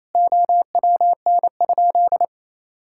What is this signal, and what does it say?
Morse code, heard as a keyed tone, about 28 wpm, 700 Hz. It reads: OWN?
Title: Sentences from Top 1000 Words - Review Part 5 - 28wpm